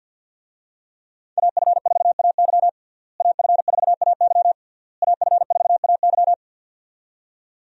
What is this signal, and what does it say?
Morse code, heard as a keyed tone, about 52 wpm, 700 Hz. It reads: AF4AY AF4AY AF4AY
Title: USA Call Signs - 5 Characters - Repeated 3x - 50wpm